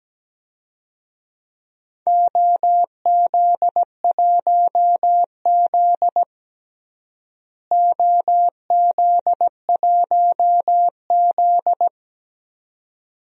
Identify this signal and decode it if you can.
Morse code, heard as a keyed tone, about 17 wpm, 700 Hz. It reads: OZ1Z OZ1Z
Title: DXCC-100 Call Signs - 4 Characters - Repeated 2x - 17wpm